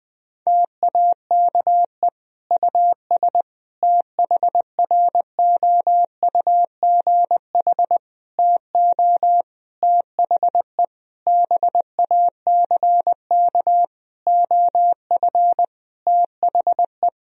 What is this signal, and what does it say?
Morse code, heard as a keyed tone, about 20 wpm, 700 Hz. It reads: TAKE US THROUGH TO THE BACK OF THE
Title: Sentences from Top 1000 Words - Review Part 1 - 20wpm